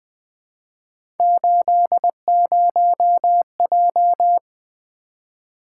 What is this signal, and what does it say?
Morse code, heard as a keyed tone, about 20 wpm, 700 Hz. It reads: 80J
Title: Three Letter-Number Combinations - 20wpm